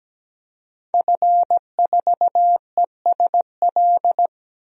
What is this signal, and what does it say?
Morse code, heard as a keyed tone, about 17 wpm, 700 Hz. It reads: F4ESL
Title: DXCC-100 Call Signs - 5 Characters - 17wpm